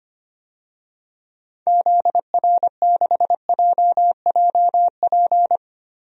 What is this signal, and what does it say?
Morse code, heard as a keyed tone, about 25 wpm, 700 Hz. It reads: ZR6JJP
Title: DXCC-100 Call Signs - 6 Characters - 25wpm